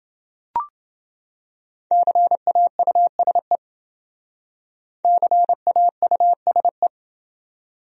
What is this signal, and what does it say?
Morse code, heard as a keyed tone, about 27 wpm, 700 Hz. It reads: CAUSE CAUSE